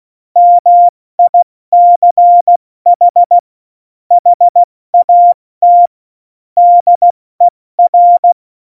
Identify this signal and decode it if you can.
Morse code, heard as a keyed tone, about 16 wpm, 700 Hz. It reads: MICH HAT DER